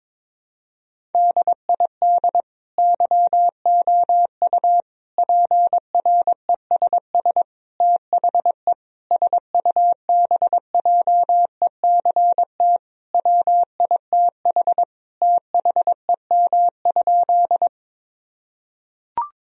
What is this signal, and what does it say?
Morse code, heard as a keyed tone, about 22 wpm, 700 Hz. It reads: DID YOU PRESS THE SUBJECT WITH THEM?
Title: Sentences from Top 1000 Words - Review Part 1 - 22wpm